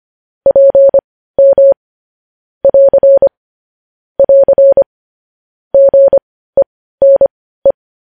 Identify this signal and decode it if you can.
Morse code, heard as a keyed tone, about 25 wpm, spaced as for 15 wpm, 550 Hz. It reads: PM <AR> <AR> GENE
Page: news headlines in morse code at 15 wpm